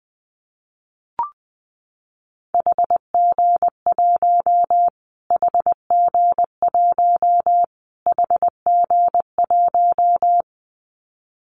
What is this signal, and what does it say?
Morse code, heard as a keyed tone, about 19 wpm, 700 Hz. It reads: HG1 HG1 HG1
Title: Callsign Prefixes for Top 25 DXCC - Repeated 3x - 20wpm